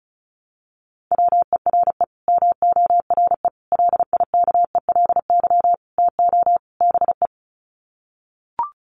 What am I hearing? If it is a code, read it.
Morse code, heard as a keyed tone, about 34 wpm, 700 Hz. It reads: WERE MORE LIKELY TO BE
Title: Sets of 5 Words Encore2 35wpm